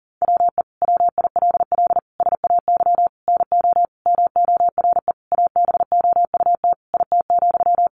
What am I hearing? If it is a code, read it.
Morse code, heard as a keyed tone, about 40 wpm, 700 Hz. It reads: WE WILL SAY NO MORE ABOUT IT,
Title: Sentences from Top 200 Words 40wpm